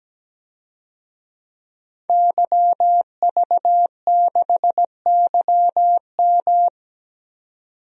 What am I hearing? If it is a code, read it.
Morse code, heard as a keyed tone, about 17 wpm, 700 Hz. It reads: YV6YM
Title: DXCC-100 Call Signs - 5 Characters - 17wpm